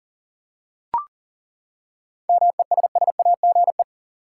Morse code, heard as a keyed tone, about 40 wpm, 700 Hz. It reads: MESSAGE
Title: Seven Letter Words - 40wpm